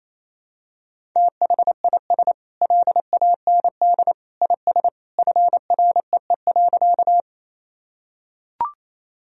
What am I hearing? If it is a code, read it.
Morse code, heard as a keyed tone, about 28 wpm, 700 Hz. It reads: THIS LAND IS FREE.
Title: Sentences from Top 1000 Words - Review Part 1 - 28wpm